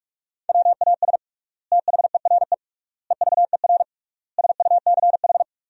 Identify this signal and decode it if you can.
Morse code, heard as a keyed tone, about 42 wpm, 700 Hz. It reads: WAS THERE EVER SUCH
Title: 3x Word-Spacing - Sentences from Top 400 Words - 45wpm